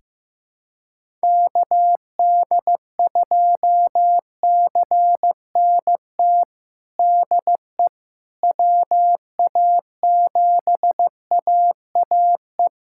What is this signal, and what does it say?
Morse code, heard as a keyed tone, about 15 wpm, 700 Hz. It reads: KD2CNT DE WA7AAE
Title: USA Callsign DE Callsign - 6 Characters - 15wpm